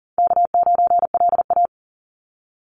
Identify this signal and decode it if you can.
Morse code, heard as a keyed tone, about 40 wpm, 700 Hz. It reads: K9LA
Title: Easy Call Signs Repeated 3x - 40wpm